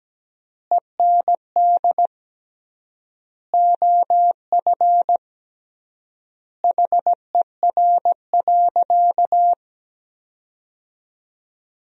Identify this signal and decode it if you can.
Morse code, heard as a keyed tone, about 17 wpm, 700 Hz. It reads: END OF HER.
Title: 3x Word-Spacing - Sentences from Top 200 Words - 17wpm